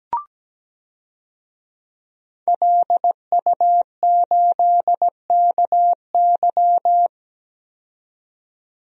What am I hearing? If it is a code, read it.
Morse code, heard as a keyed tone, about 17 wpm, 700 Hz. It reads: LU8KY
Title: DXCC-100 Call Signs - 5 Characters - 17wpm